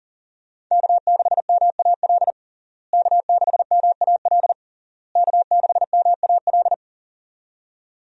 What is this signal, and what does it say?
Morse code, heard as a keyed tone, about 40 wpm, 700 Hz. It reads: K6MAL K6MAL K6MAL
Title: USA Call Signs - 5 Characters - Repeated 3x - 40wpm